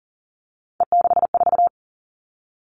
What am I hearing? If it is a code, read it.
Morse code, heard as a keyed tone, about 40 wpm, 690 Hz. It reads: E64
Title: Three Challenging Letter-Number Combinations 40wpm